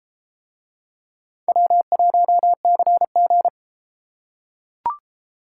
Morse code, heard as a keyed tone, about 31 wpm, 700 Hz. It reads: W1CG